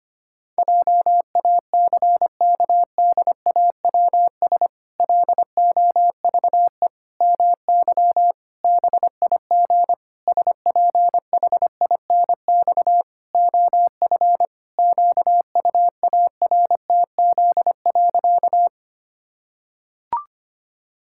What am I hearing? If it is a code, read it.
Morse code, heard as a keyed tone, about 25 wpm, 700 Hz. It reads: JACKDAWS LOVE MY BIG SPHINX OF QUARTZ.
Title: Pangram Challenge v2 - 25wpm